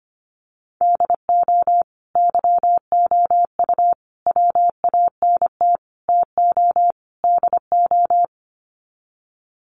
Morse code, heard as a keyed tone, about 25 wpm, 700 Hz. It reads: DO YOU WANT TO DO
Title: Sets of 5 Words Encore 25wpm